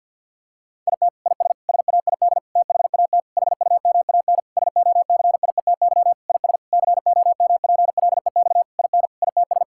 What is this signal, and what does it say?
Morse code, heard as a keyed tone, about 50 wpm, 700 Hz. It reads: IT IS SAID THAT HUMAN SOCIETY IS COMPLEX IN ITS
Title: Sentences from Top 1000 Words - Review Part 6 - 50wpm